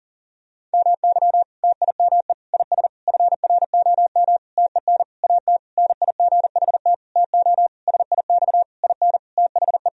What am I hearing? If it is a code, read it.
Morse code, heard as a keyed tone, about 40 wpm, 700 Hz. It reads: MY TIME IS FROM TEN AT NIGHT TO SIX IN THE
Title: Sentences from Top 1000 Words - Review Part 2 - 40wpm